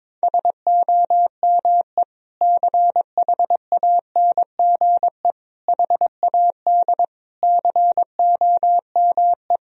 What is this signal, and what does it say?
Morse code, heard as a keyed tone, about 22 wpm, 700 Hz. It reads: SOME CHANGE HAD COME
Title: Sentences from Top 1000 Words - Review Part 2 - 22wpm